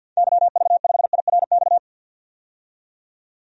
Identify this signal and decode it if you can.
Morse code, heard as a keyed tone, about 51 wpm, 700 Hz. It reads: YV5IRX
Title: DXCC-100 Call Signs - 6 Characters - 50wpm